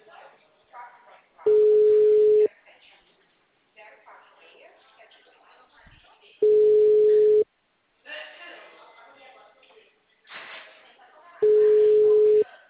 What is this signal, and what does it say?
Ringback tone (Europe, 425 Hz), with speech.
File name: out- -2007- - - .572